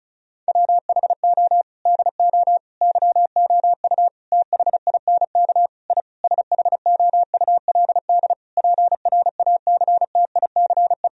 Morse code, heard as a keyed tone, about 36 wpm, 700 Hz. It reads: WHO DO YOU THINK I SHOULD PRACTICE
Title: Sentences from Top 1000 Words - Review Part 4 - 35wpm